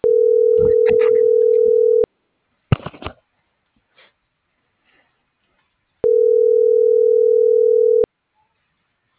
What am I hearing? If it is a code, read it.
Ringback tone (North America), 440 + 480 Hz.